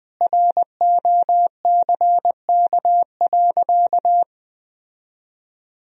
Morse code, heard as a keyed tone, about 20 wpm, 700 Hz. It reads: ROCK.
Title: Sentences from Top 1000 Words - Review Part 2 - 20wpm